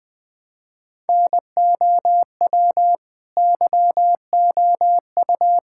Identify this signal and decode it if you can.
Morse code, heard as a keyed tone, about 20 wpm, 700 Hz. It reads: NOW YOU
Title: Sentences from Top 1000 Words - Review Part 6 - 20wpm